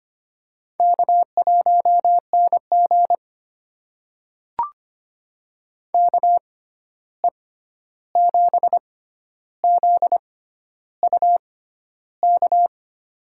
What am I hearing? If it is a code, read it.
Morse code, heard as a keyed tone, about 25 wpm, spaced as for 11 wpm, 700 Hz. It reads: K1NG KE7ZUK
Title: USA Call Signs - 6x Character Spacing - 25wpm